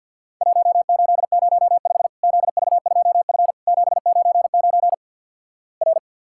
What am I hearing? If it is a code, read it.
Morse code, heard as a keyed tone, about 50 wpm, 700 Hz. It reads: 1805 ZV2F 799 R
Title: Worked All Europe DX Contest - QTC Exchange - 50wpm